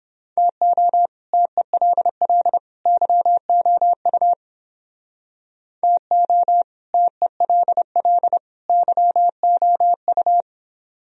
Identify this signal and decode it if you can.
Morse code, heard as a keyed tone, about 27 wpm, 700 Hz. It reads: TO TELL YOU TO TELL YOU
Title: Speed-Racing - Sets of 3 Words 22wpm